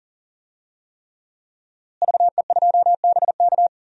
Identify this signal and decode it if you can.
Morse code, heard as a keyed tone, about 40 wpm, 700 Hz. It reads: VE2BK
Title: DXCC-100 Call Signs - 5 Characters - Repeated 3x - 40wpm